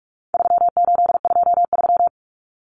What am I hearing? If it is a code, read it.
Morse code, heard as a keyed tone, about 44 wpm, 700 Hz. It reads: 3823
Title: Common Cut Numbers - 4-Digit 45wpm